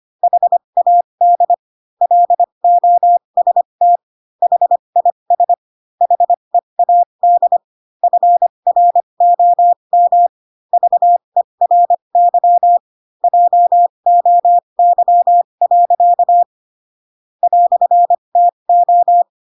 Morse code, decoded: HAD LOST HIS HEAD FROM VERY JOY. "TO

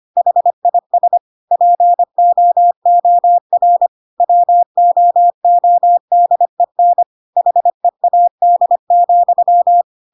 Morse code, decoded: HIS POOR WOODEN HEAD,

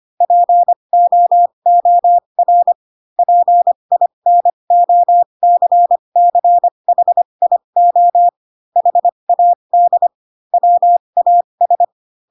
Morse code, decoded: POOR PINOCCHIO HAD WAS

